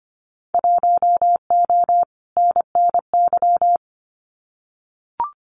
Morse code, heard as a keyed tone, about 25 wpm, 700 Hz. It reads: 1O NNY
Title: Winter Field Day Exchanges Encore 25wpm